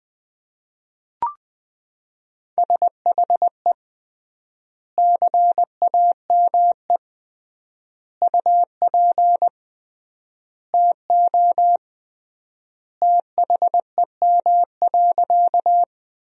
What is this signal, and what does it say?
Morse code, heard as a keyed tone, about 20 wpm, 700 Hz. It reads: SHE CAME UP TO THEM.
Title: 3x Word-Spacing - Sentences from Top 200 Words - 20wpm